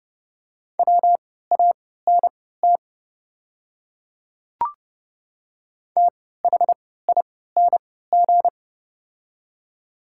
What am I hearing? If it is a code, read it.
Morse code, heard as a keyed tone, about 30 wpm, spaced as for 17 wpm, 700 Hz. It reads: WANT THING
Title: 3x Character Spacing - Top 200 Words - 30wpm